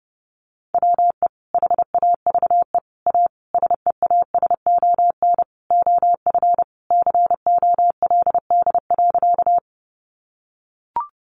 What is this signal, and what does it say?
Morse code, heard as a keyed tone, about 30 wpm, 700 Hz. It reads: WE HAVE A SEASON OF COLD.
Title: Sentences from Top 1400 Words 30wpm